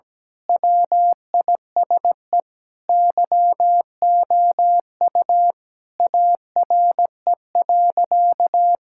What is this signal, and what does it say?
Morse code, heard as a keyed tone, about 17 wpm, 700 Hz. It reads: WISE YOU ARE.
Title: Sentences from Top 1600 Words 17wpm